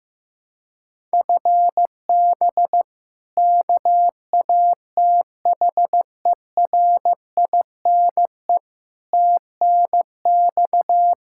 Morse code, heard as a keyed tone, about 15 wpm, 700 Hz. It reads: FB KATHERINE TNX
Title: K1USN SST QSO Practice 15wpm